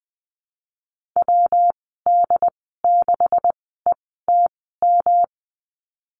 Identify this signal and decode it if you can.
Morse code, heard as a keyed tone, about 20 wpm, spaced as for 15 wpm, 700 Hz. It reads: WD6ETM
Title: 147 Lesson 24g - Letter y - Call Signs 20wpm